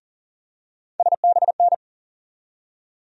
Morse code, heard as a keyed tone, about 40 wpm, 700 Hz. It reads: SBN
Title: USA Call Sign Postfixes - 3 Letters 40wpm